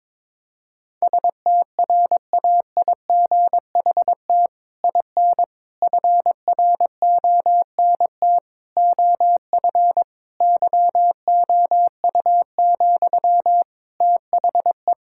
Morse code, heard as a keyed tone, about 22 wpm, 700 Hz. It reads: STRAIGHT IN FRONT OF YOU, THE
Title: Sentences from Top 1000 Words - Review Part 6 - 22wpm